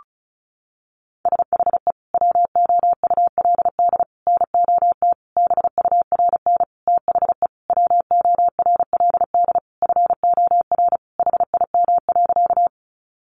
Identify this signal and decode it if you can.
Morse code, heard as a keyed tone, about 34 wpm, 700 Hz. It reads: SHE WOULD NOT BURN THE WORLD FOR HIM.
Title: Sentences from Top 1300 Words 35wpm